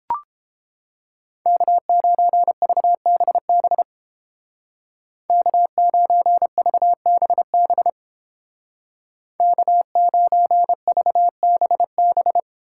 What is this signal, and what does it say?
Morse code, heard as a keyed tone, about 30 wpm, 700 Hz. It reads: K9VBB K9VBB K9VBB